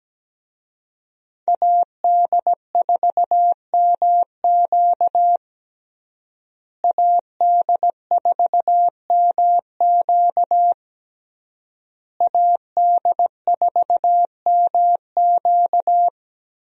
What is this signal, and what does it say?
Morse code, heard as a keyed tone, about 17 wpm, 700 Hz. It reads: AD4MQ AD4MQ AD4MQ
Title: USA Call Signs - 5 Characters - Repeated 3x - 17wpm